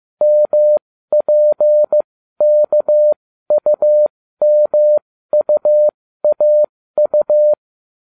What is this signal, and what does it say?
Morse code, heard as a keyed tone, about 15 wpm, spaced as for 13 wpm, 600 Hz. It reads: MPKUMUAU